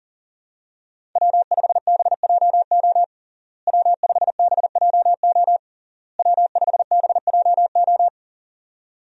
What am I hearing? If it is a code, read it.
Morse code, heard as a keyed tone, about 40 wpm, 700 Hz. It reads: W5BJO W5BJO W5BJO
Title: USA Call Signs - 5 Characters - Repeated 3x - 40wpm